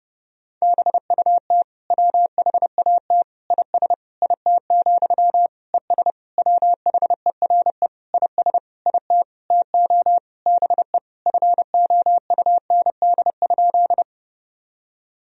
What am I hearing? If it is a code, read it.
Morse code, heard as a keyed tone, about 32 wpm, 700 Hz. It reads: BUT WHAT IS IT, ES WHERE IS IT TO BE FOUND?
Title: Sentences from Top 300 Words and Abbreviations 30wpm